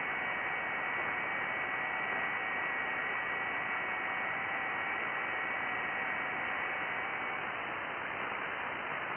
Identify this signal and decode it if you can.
FT8.
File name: IC-7100